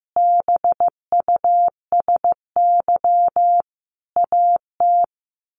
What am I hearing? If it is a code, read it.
Morse code, heard as a keyed tone, about 15 wpm, 700 Hz. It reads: BUSY AT
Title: A Sound Of Thunder 15wpm